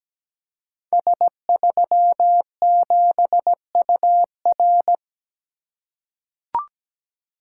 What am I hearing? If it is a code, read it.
Morse code, heard as a keyed tone, about 17 wpm, 700 Hz. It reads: S37UR
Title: Code Groups - 5 Letter:Numbers - 17wpm